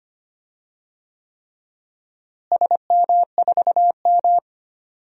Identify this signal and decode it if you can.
Morse code, heard as a keyed tone, about 25 wpm, 700 Hz. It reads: SM4M
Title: DXCC-100 Call Signs - 4 Characters - Repeated 3x - 25wpm